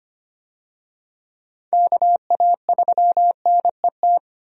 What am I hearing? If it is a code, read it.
Morse code, heard as a keyed tone, about 25 wpm, 700 Hz. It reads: KA3NET
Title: USA Call Signs - 6 Characters - 25wpm